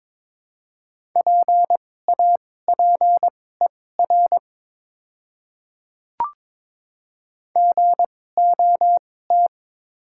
Morse code, heard as a keyed tone, about 22 wpm, spaced as for 16 wpm, 690 Hz. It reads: PAPER GOT
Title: 2x Character Spacing - Top 300 Words - 22wpm